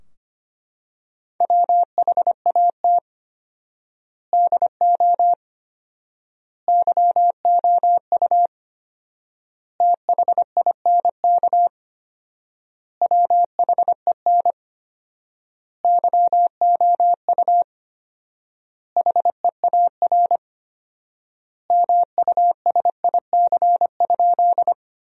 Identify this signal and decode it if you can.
Morse code, heard as a keyed tone, about 25 wpm, 700 Hz. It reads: WHAT DO YOU THINK WHEN YOU HEAR MUSIC?
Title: 4x Word-Spacing - Sentences from Top 300 Words - 25wpm